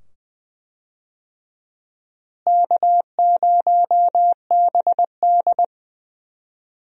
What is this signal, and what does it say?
Morse code, heard as a keyed tone, about 20 wpm, 700 Hz. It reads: K0BD
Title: USA Call Signs - 4 Characters - Repeated 2x - 20wpm